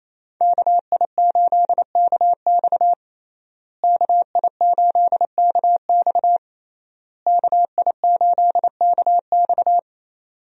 Morse code, heard as a keyed tone, about 28 wpm, 700 Hz. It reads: KI8KX KI8KX KI8KX
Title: USA Call Signs - 5 Characters - Repeated 3x - 28wpm